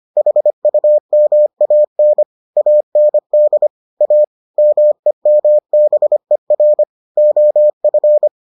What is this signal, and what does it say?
Morse code, heard as a keyed tone, about 25 wpm, 600 Hz. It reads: HUMAN AND A MEMBER OF